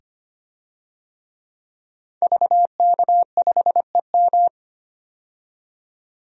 Morse code, heard as a keyed tone, about 25 wpm, 700 Hz. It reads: VK5EM
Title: DXCC-100 Call Signs - 5 Characters - 25wpm